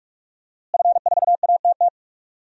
Morse code, heard as a keyed tone, about 46 wpm, 700 Hz. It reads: F4ATT
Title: DXCC-100 Call Signs - 5 Characters - Repeated 3x - 45wpm